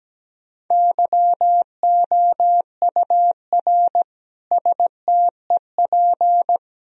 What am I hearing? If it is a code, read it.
Morse code, heard as a keyed tone, about 17 wpm, 700 Hz. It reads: YOUR STEP
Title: Sentences from Top 1000 Words - Review Part 2 - 17wpm